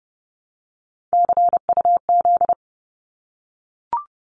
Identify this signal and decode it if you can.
Morse code, heard as a keyed tone, about 30 wpm, 700 Hz. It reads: CUZ
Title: Abbreviations 30wpm